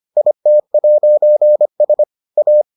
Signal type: Morse code

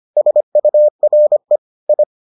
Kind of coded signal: Morse code

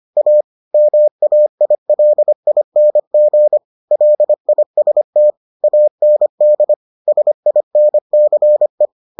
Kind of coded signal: Morse code

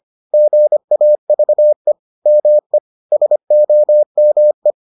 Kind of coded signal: Morse code

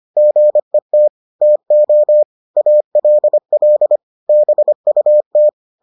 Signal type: Morse code

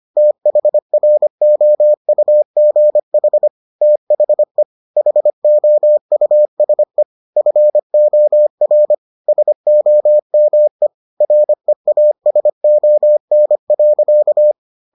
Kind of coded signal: Morse code

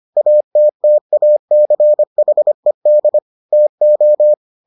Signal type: Morse code

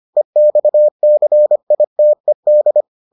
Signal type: Morse code

